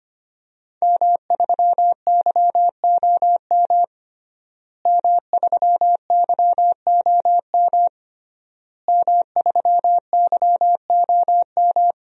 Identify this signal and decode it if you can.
Morse code, heard as a keyed tone, about 25 wpm, 700 Hz. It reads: M3YOM M3YOM M3YOM
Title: DXCC-100 Call Signs - 5 Characters - Repeated 3x - 25wpm